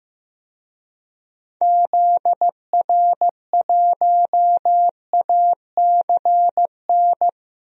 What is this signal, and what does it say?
Morse code, heard as a keyed tone, about 15 wpm, 700 Hz. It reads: ZR1ACN
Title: DXCC-100 Call Signs - 6 Characters - Repeated 3x - 15wpm